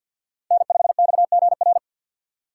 Morse code, heard as a keyed tone, about 51 wpm, 700 Hz. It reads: N5XGR
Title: USA Call Signs - 5 Characters - 3x Character Spacing - 50wpm